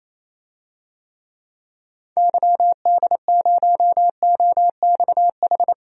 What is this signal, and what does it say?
Morse code, heard as a keyed tone, about 28 wpm, 700 Hz. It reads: YD0OXH
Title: DXCC-100 Call Signs - 6 Characters - Repeated 2x - 28wpm